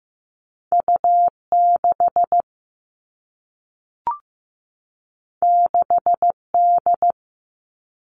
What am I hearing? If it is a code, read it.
Morse code, heard as a keyed tone, about 15 wpm, 700 Hz. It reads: U6 6D
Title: All Cut Numbers - 2-Digit 15wpm